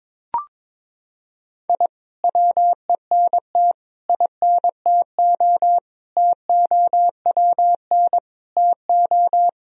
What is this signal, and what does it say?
Morse code, heard as a keyed tone, about 22 wpm, 700 Hz. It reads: I WENT INTO TOWN TO
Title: Sentences from Top 1000 Words - Review Part 4 - 22wpm